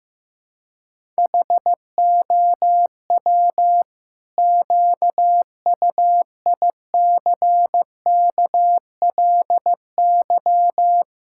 Morse code, heard as a keyed tone, about 15 wpm, 700 Hz. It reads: HOW QUICKLY
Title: Sentences from Top 1000 Words - Review Part 5 - 15wpm